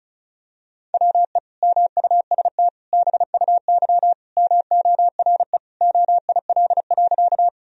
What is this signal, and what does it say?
Morse code, heard as a keyed tone, about 36 wpm, 700 Hz. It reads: WE MUST BUY MORE OIL.
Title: Sentences from Top 1000 Words - Review Part 4 - 35wpm